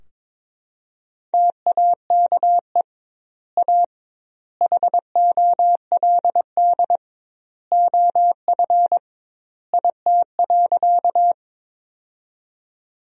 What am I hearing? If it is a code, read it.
Morse code, heard as a keyed tone, about 22 wpm, 700 Hz. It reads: TAKE A HOLD OF IT.
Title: 2x Word-Spacing - Sentences from Top 400 Words - 22wpm